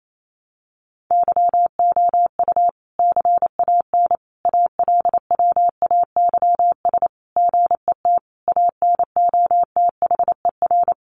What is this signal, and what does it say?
Morse code, heard as a keyed tone, about 28 wpm, 700 Hz. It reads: YOU CAN ALWAYS GET ANOTHER
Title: Sentences from Top 1200 Words 28wpm